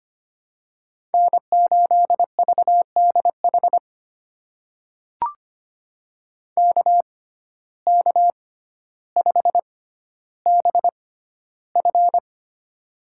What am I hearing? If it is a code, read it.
Morse code, heard as a keyed tone, about 25 wpm, spaced as for 13 wpm, 700 Hz. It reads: N8VDH KK5BF
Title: USA Call Signs - 6x Character Spacing - 25wpm